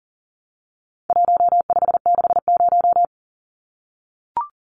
Morse code, heard as a keyed tone, about 40 wpm, 700 Hz. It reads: 1560
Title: Numbers - Contest Style - 4-Digit 40wpm